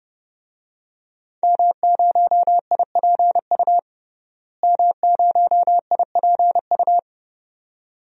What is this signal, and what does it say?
Morse code, heard as a keyed tone, about 31 wpm, 700 Hz. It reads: M0IPU M0IPU
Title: DXCC-100 Call Signs - 5 Characters - Repeated 2x - 30wpm